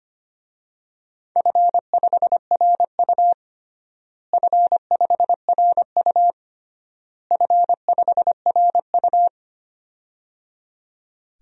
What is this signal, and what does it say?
Morse code, heard as a keyed tone, about 25 wpm, 700 Hz. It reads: F5RU F5RU F5RU
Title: DXCC-100 Call Signs - 4 Characters - Repeated 3x - 25wpm